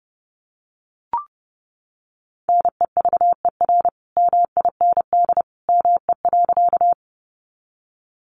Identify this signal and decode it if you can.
Morse code, heard as a keyed tone, about 30 wpm, 700 Hz. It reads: NEVER MIND ME.
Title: Sentences from Top 500 Words 30wpm